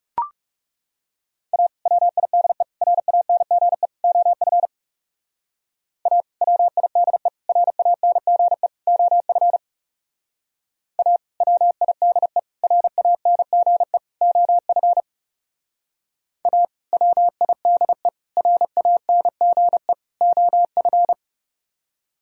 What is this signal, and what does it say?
Morse code, heard as a keyed tone, about 36 wpm, 700 Hz. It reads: A WIDE RANGE OF A WIDE RANGE OF A WIDE RANGE OF A WIDE RANGE OF
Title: Speed-Racing - Sets of 4 Words 30wpm